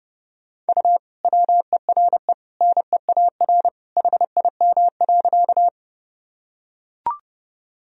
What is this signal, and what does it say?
Morse code, heard as a keyed tone, about 30 wpm, 700 Hz. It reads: U WERE NEAR HIM.
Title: Sentences from Top 200 Words and Abbreviations 30wpm